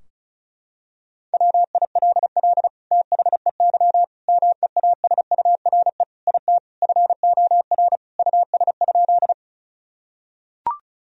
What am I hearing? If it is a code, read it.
Morse code, heard as a keyed tone, about 34 wpm, 700 Hz. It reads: WILL THEY MEASURE IT FOR US?
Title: Sentences from Top 1000 Words - Review Part 2 - 35wpm